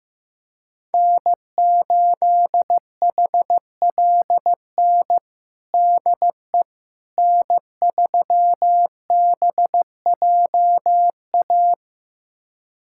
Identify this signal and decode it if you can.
Morse code, heard as a keyed tone, about 15 wpm, 700 Hz. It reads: N8HLN DE N3BJA
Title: USA Callsign DE Callsign - 5 Characters - 15wpm